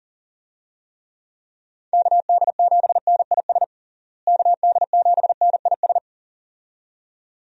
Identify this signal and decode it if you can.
Morse code, heard as a keyed tone, about 40 wpm, 700 Hz. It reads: KD7NIS KD7NIS
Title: USA Call Signs - 6 Characters - Repeated 2x - 40wpm